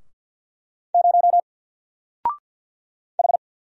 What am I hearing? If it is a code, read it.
Morse code, heard as a keyed tone, about 47 wpm, 700 Hz. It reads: 0 H
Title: Single Letter-Number Alt 50wpm